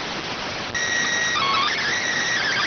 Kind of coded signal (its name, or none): SSTV (slow-scan television)